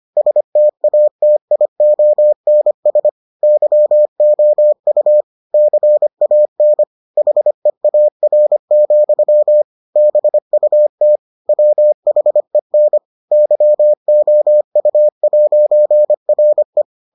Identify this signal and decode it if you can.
Morse code, heard as a keyed tone, about 25 wpm, 600 Hz. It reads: STATIONS YOU CAN HEAR, BUT WHEN YOU'RE